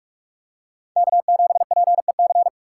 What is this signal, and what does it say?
Morse code, heard as a keyed tone, about 44 wpm, 700 Hz. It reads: K7PEC